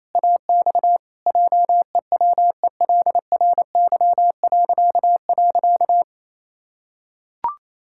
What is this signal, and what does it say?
Morse code, heard as a keyed tone, about 27 wpm, 700 Hz. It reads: AX JEWELRY..
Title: Pangram Challenge v2 - 28wpm